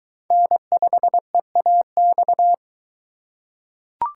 Morse code, heard as a keyed tone, about 23 wpm, 700 Hz. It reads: N5EAX